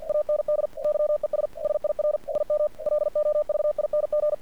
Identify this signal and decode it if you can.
Morse code, heard as a keyed tone, about 46 wpm, 600 Hz. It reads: AND YES SIR IM LOVING
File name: archived / CRSnet